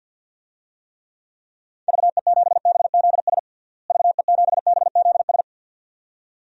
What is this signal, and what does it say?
Morse code, heard as a keyed tone, about 53 wpm, 700 Hz. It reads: VE7BZS VE7BZS
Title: DXCC-100 Call Signs - 6 Characters - Repeated 2x - 50wpm